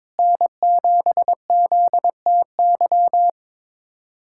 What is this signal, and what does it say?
Morse code, heard as a keyed tone, about 21 wpm, 700 Hz. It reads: N7ZTY
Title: USA Callsign DE Callsign - 5 Characters - 22wpm